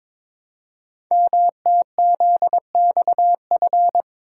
Morse code, heard as a keyed tone, about 22 wpm, 700 Hz. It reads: MTZXF
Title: Code Groups - 5 Letters - 22wpm